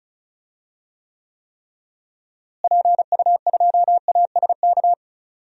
Morse code, heard as a keyed tone, about 36 wpm, 700 Hz. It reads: PU2ASK
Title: DXCC-100 Call Signs - 6 Characters - Repeated 3x - 35wpm